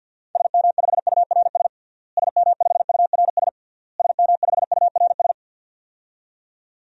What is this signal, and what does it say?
Morse code, heard as a keyed tone, about 50 wpm, 700 Hz. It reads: SM5URS SM5URS SM5URS
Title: DXCC-100 Call Signs - 6 Characters - Repeated 3x - 50wpm